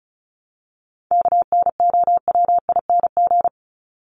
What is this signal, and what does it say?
Morse code, heard as a keyed tone, about 34 wpm, 700 Hz. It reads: KNOWING
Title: ING Suffix - Top 100 Words with Base Words 35wpm